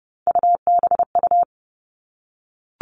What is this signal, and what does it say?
Morse code, heard as a keyed tone, about 30 wpm, 700 Hz. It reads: UBU
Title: All Cut Numbers - 3-Digit 30wpm